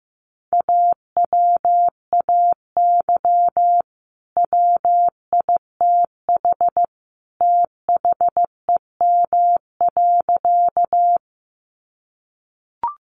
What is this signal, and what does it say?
Morse code, heard as a keyed tone, about 15 wpm, 700 Hz. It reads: AWAY WITH THEM.
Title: Sentences from Top 600 Words 15wpm